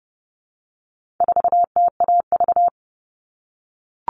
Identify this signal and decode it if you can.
Morse code, heard as a keyed tone, about 30 wpm, 700 Hz. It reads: 4TAV
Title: All Cut Numbers - 4-Digit 30wpm